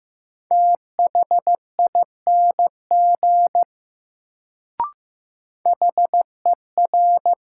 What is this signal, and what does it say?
Morse code, heard as a keyed tone, about 15 wpm, 700 Hz. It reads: THING HER
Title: Sentences from Top 100 Words - New Word Drill - 15wpm